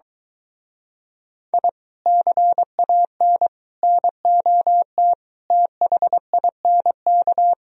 Morse code, heard as a keyed tone, about 23 wpm, 700 Hz. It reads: I CAN NOT THINK